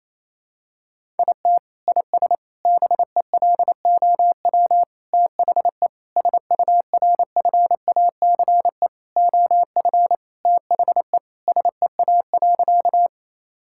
Morse code, heard as a keyed tone, about 28 wpm, 700 Hz. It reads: IT IS BELOW THE SURFACE OF THE SEA.
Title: Sentences from Top 1000 Words - Review Part 5 - 28wpm